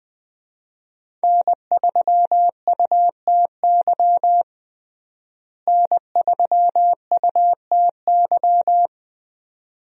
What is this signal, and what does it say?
Morse code, heard as a keyed tone, about 20 wpm, 700 Hz. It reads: N3UTY N3UTY
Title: USA Call Signs - 5 Characters - Repeated 2x - 20wpm